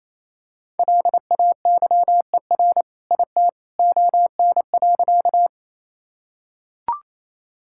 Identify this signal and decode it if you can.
Morse code, heard as a keyed tone, about 28 wpm, 700 Hz. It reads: LAYER IT ON.
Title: Sentences from Top 1600 Words 28wpm